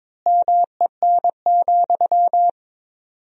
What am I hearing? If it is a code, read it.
Morse code, heard as a keyed tone, about 21 wpm, 700 Hz. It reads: MEN,
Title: 4x Word-Spacing - Sentences from Top 500 Words - 22wpm